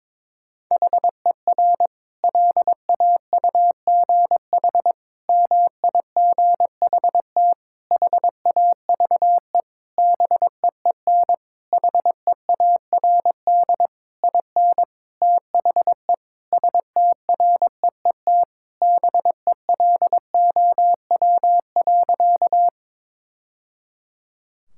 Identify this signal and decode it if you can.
Morse code, heard as a keyed tone, about 22 wpm, 700 Hz. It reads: HER LAUGH MIGHT HAVE BEEN HEARD IN THE STREET BELOW.
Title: Sentences from Top 1000 Words - Review Part 5 - 22wpm